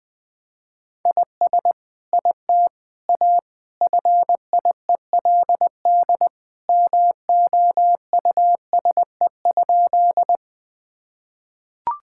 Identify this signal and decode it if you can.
Morse code, heard as a keyed tone, about 20 wpm, 700 Hz. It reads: IS IT A FIELD MOUSE?
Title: Sentences from Top 1000 Words - Review Part 5 - 20wpm